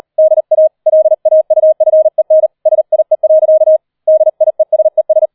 Morse code, heard as a keyed tone, about 38 wpm, 630 Hz. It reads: DA LAUFEN SIE. DIESES